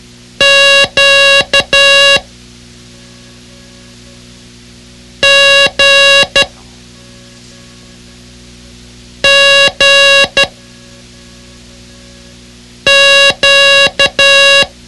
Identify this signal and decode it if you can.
Morse code, heard as a keyed tone, about 12 wpm, 560 Hz. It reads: Q G G Q
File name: ham radio / CodeQuick / Code Quick 06